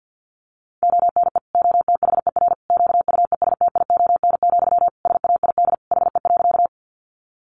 Morse code, heard as a keyed tone, about 50 wpm, 700 Hz. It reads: ONE OTHER QUESTION, SAID HE.
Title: Sentences from Top 400 Words 50wpm